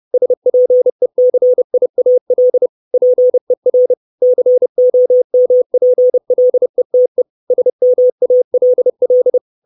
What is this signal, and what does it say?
Morse code, heard as a keyed tone, about 30 wpm, 500 Hz. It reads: SPECIAL PER COMPLETE SMALL
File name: 632 Words 30WPM